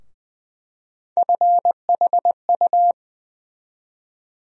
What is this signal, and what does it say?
Morse code, heard as a keyed tone, about 20 wpm, 700 Hz. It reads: FHU
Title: USA Call Sign Postfixes - 3 Letters 20wpm